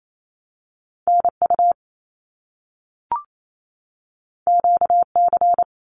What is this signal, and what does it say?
Morse code, heard as a keyed tone, about 28 wpm, 700 Hz. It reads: NU QC
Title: Canadian Provinces and Territories 28wpm